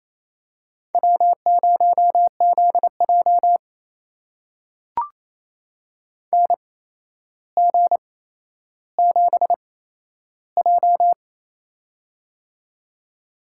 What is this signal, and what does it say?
Morse code, heard as a keyed tone, about 27 wpm, spaced as for 12 wpm, 700 Hz. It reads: W0ZJ NG7J
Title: USA Call Signs - 4 Characters - 8x Character Spacing - 28wpm